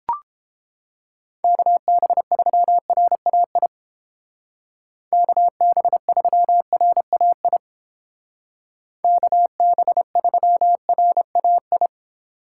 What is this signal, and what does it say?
Morse code, heard as a keyed tone, about 29 wpm, 700 Hz. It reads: KB3RAI KB3RAI KB3RAI